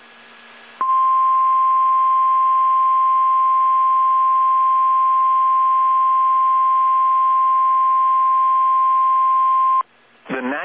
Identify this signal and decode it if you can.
Emergency Alert System (SAME) tones (the 1050 Hz attention tone), with speech. It is heard from 0.8 to 9.9 s.